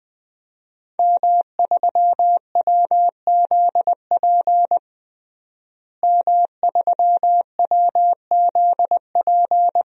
Morse code, heard as a keyed tone, about 21 wpm, 700 Hz. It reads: M3WZP M3WZP
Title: DXCC-100 Call Signs - 5 Characters - Repeated 2x - 20wpm